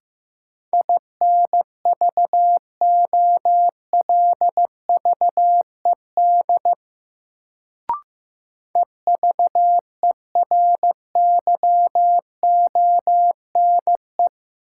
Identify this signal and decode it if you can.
Morse code, heard as a keyed tone, about 16 wpm, 700 Hz. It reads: INVOLVED EVERYONE
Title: Sentences from Top 800 Words - New Word Drill - 15wpm